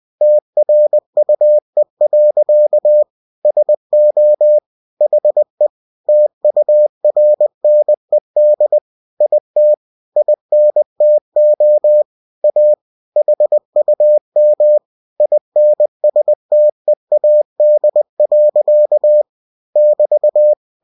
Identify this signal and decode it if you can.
Morse code, heard as a keyed tone, about 20 wpm, 600 Hz. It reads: TRUE. SO HE TURNED IT INTO A HUM INSTEAD. <BT>
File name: cw-books / winnie-the-pooh / winnie-20wpm-page-0076